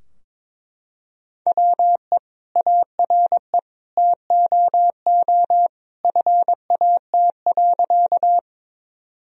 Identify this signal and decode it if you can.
Morse code, heard as a keyed tone, about 22 wpm, 700 Hz. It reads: WE ARE TOO FAT.
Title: Sentences from Top 1000 Words - Review Part 4 - 22wpm